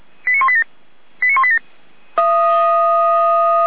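Five-tone selective call (ZVEI), heard as quick runs of five tones, tones of about 70 ms. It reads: 89187 89187
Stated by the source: ZVEI five-tone call 89050, received on SDR.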